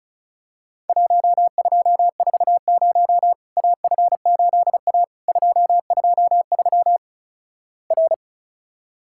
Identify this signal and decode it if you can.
Morse code, heard as a keyed tone, about 36 wpm, 700 Hz. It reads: 1240 AF8A 223 R
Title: Worked All Europe DX Contest - QTC Exchange - 35wpm